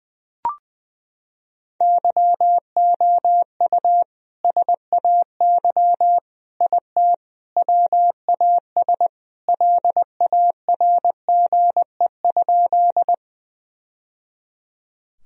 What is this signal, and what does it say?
Morse code, heard as a keyed tone, about 20 wpm, 700 Hz. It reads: YOU SAY IT WAS LARGE?
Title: Sentences from Top 1000 Words - Review Part 1 - 20wpm